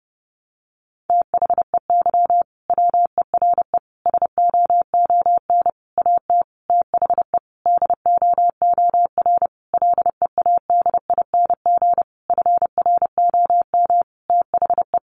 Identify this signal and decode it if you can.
Morse code, heard as a keyed tone, about 28 wpm, 700 Hz. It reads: THEY WERE SOON AT THE DOOR LEADING FROM THE
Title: Sentences from Top 1200 Words 30wpm